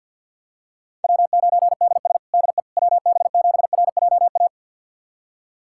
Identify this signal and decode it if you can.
Morse code, heard as a keyed tone, about 52 wpm, 700 Hz. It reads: W9DS DE WB7RJA
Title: USA Callsign DE Callsign 50wpm